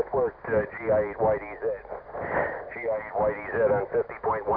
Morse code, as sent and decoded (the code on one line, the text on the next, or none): none